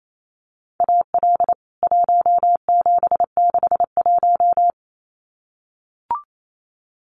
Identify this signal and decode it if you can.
Morse code, heard as a keyed tone, about 28 wpm, 700 Hz. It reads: AL 1761
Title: CWT Exchange 28wpm